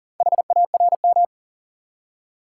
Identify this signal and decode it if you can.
Morse code, heard as a keyed tone, about 40 wpm, 700 Hz. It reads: HARM
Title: Four Letter Words - 40wpm